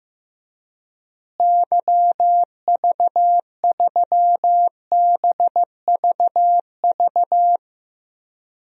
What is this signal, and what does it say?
Morse code, heard as a keyed tone, about 15 wpm, 700 Hz. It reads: YV3BVV
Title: DXCC-100 Call Signs - 6 Characters - Repeated 3x - 15wpm